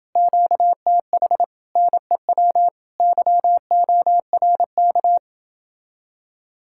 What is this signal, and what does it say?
Morse code, heard as a keyed tone, about 29 wpm, 700 Hz. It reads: QTH NEW YORK